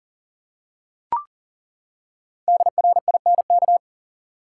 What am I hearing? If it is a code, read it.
Morse code, heard as a keyed tone, about 37 wpm, 700 Hz. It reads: DRINK
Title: Five Letter Words - 40wpm